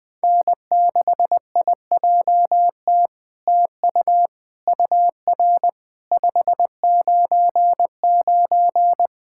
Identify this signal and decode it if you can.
Morse code, heard as a keyed tone, about 20 wpm, 700 Hz. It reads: N6IJT TU UR 599
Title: SOTA QSO Practice 20wpm